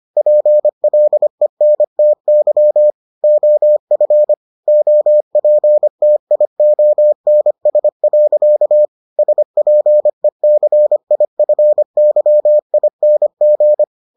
Morse code, heard as a keyed tone, about 25 wpm, 600 Hz. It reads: PLENTY OF OPTIONS. SPECIFYING